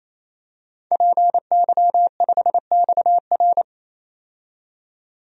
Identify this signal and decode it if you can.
Morse code, heard as a keyed tone, about 27 wpm, 700 Hz. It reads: PY5XR
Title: DXCC-100 Call Signs - 5 Characters - 28wpm